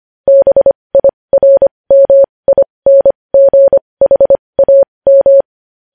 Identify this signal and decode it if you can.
Morse code, heard as a keyed tone, about 25 wpm, spaced as for 20 wpm, 550 Hz. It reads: BIRMINGHAM
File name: podcast / News headlines-1x-25x20